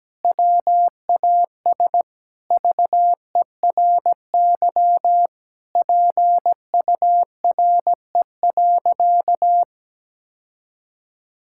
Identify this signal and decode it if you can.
Morse code, heard as a keyed tone, about 17 wpm, 700 Hz. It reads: WAS VERY PURE.
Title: Sentences from Top 1600 Words 17wpm